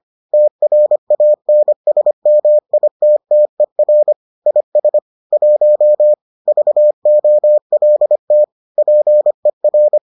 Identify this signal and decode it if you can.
Morse code, heard as a keyed tone, about 25 wpm, 600 Hz. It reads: TRANSMITTER IS 1 VOLT PER